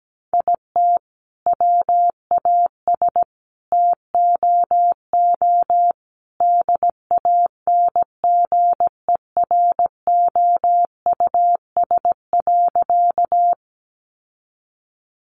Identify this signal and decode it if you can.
Morse code, heard as a keyed tone, about 17 wpm, 700 Hz. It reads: IT WAS TOO DANGEROUS.
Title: Sentences from Top 1200 Words 17wpm